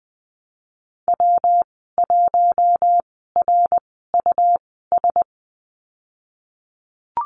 Morse code, heard as a keyed tone, about 19 wpm, spaced as for 14 wpm, 700 Hz. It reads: W1RUS
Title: 122 Lesson 20g - Number 3 - Call Signs 20wpm